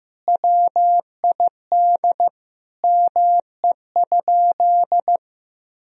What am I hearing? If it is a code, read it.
Morse code, heard as a keyed tone, about 15 wpm, 700 Hz. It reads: WID ME?
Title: Sentences from Top 200 Words and Abbreviations 15wpm